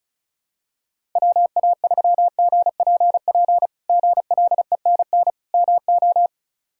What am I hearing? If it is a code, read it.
Morse code, heard as a keyed tone, about 34 wpm, 700 Hz. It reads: WA3GPP GLENN MO
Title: K1USN SST QSO Practice - 35wpm